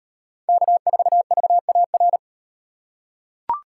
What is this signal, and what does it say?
Morse code, heard as a keyed tone, about 38 wpm, 700 Hz. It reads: K4VAR